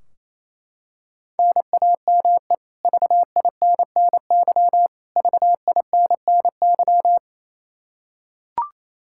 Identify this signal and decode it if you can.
Morse code, heard as a keyed tone, about 28 wpm, 700 Hz. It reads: NAME VINNY VINNY
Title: Practice QSO - Exchange 1 - Beginning - 28wpm